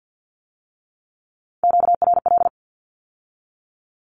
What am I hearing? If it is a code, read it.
Morse code, heard as a keyed tone, about 50 wpm, 700 Hz. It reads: QRL
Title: Q-Codes with Spoken Meaning - 50wpm